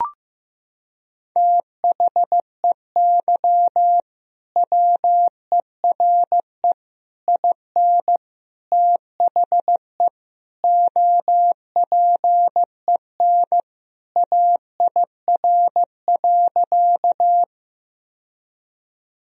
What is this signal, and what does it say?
Morse code, heard as a keyed tone, about 16 wpm, 700 Hz. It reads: THEY WERE IN THE OPEN AIR.
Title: Sentences from Top 1000 Words - Review Part 1 - 15wpm